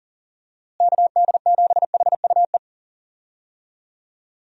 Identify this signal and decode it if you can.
Morse code, heard as a keyed tone, about 39 wpm, 700 Hz. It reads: KD7HUE
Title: USA Call Signs - 6 Characters - Repeated 3x - 40wpm